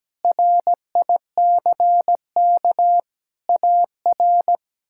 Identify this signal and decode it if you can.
Morse code, heard as a keyed tone, about 17 wpm, 700 Hz. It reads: RICK AR
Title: K1USN SST QSO Practice - 17wpm